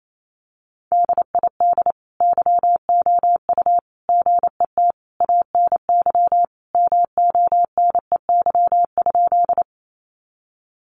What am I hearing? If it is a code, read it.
Morse code, heard as a keyed tone, about 28 wpm, 700 Hz. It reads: DID YOU GET ANY MONEY?
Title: Sentences from Top 400 Words 28wpm